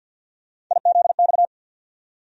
Morse code, heard as a keyed tone, about 50 wpm, 700 Hz. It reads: IZX
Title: USA Call Sign Postfixes - 3 Letters 50wpm